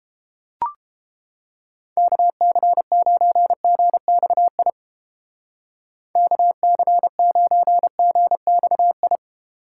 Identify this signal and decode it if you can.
Morse code, heard as a keyed tone, about 31 wpm, 700 Hz. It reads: KC9GXI KC9GXI